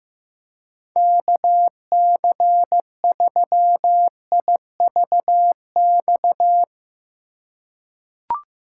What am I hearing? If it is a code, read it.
Morse code, heard as a keyed tone, about 15 wpm, 700 Hz. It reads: KC3IVX
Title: USA Call Signs - 6 Characters - 2x Character Spacing - 15wpm